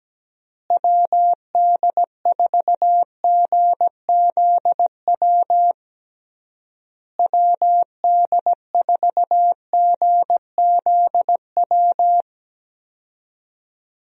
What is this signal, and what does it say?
Morse code, heard as a keyed tone, about 17 wpm, 700 Hz. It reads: WD4GZW WD4GZW
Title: USA Call Signs - 6 Characters - Repeated 3x - 17wpm